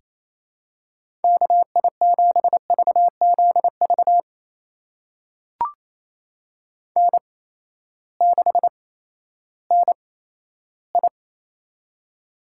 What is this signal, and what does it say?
Morse code, heard as a keyed tone, about 28 wpm, spaced as for 14 wpm, 700 Hz. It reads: KI7VZV N6NI
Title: USA Call Signs - 8x Character Spacing - 28wpm